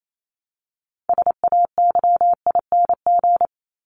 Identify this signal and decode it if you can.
Morse code, heard as a keyed tone, about 28 wpm, 700 Hz. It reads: SAYING
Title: ING Suffix - Top 100 Words with Base Words 28wpm